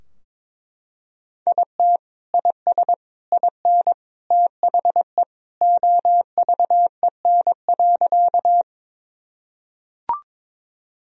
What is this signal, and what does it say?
Morse code, heard as a keyed tone, about 22 wpm, 700 Hz. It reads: IT IS IN THE OVEN.
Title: Sentences from Top 1000 Words - Review Part 4 - 22wpm